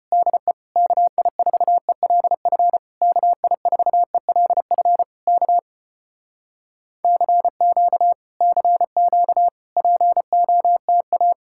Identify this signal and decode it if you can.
Morse code, heard as a keyed tone, about 35 wpm, 700 Hz. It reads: DE KI4ELF KI4ELF K CQ CQ POTA